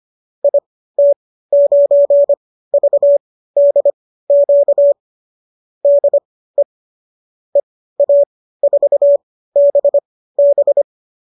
Morse code, heard as a keyed tone, about 25 wpm, spaced as for 15 wpm, 570 Hz. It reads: IT9VDQ DE EA4BB